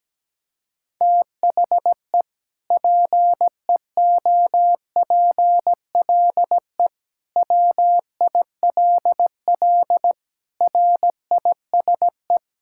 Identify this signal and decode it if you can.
Morse code, heard as a keyed tone, about 17 wpm, 700 Hz. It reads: THE PEOPLE WILL RISE
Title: Sentences from Top 1000 Words - Review Part 6 - 17wpm